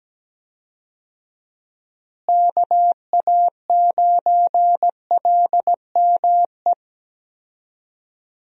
Morse code, heard as a keyed tone, about 17 wpm, 700 Hz. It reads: KA9LME
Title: USA Call Signs - 6 Characters - Repeated 3x - 17wpm